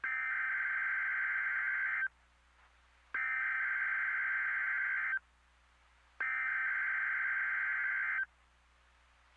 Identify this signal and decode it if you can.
Emergency Alert System (SAME) tones (header bursts), heard as harsh buzzing bursts. It is heard from 0.0 to 8.3 s.
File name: NOAA Weather Radio MKE-KEC60 Weekly Test